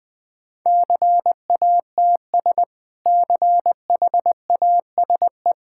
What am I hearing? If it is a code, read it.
Morse code, heard as a keyed tone, about 20 wpm, 700 Hz. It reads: CATS CHASE
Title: Easy Sentences with Hints - 20wpm